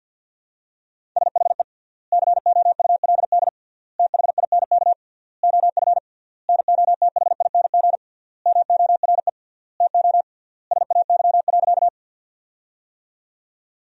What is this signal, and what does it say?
Morse code, heard as a keyed tone, about 47 wpm, 700 Hz. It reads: SHE COULD THINK OF NOTHING MORE TO SAY.
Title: 3x Word-Spacing - Sentences from Top 500 Words - 50wpm